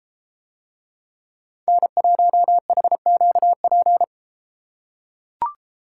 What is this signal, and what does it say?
Morse code, heard as a keyed tone, about 32 wpm, 700 Hz. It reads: N1HQP